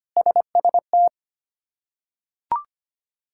Morse code, heard as a keyed tone, about 25 wpm, 700 Hz. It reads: SST
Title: K1USN SST QSO Practice - 25wpm